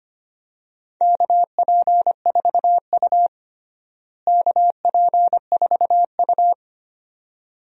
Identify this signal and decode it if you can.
Morse code, heard as a keyed tone, about 25 wpm, 700 Hz. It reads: KP4U KP4U
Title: DXCC-100 Call Signs - 4 Characters - Repeated 2x - 25wpm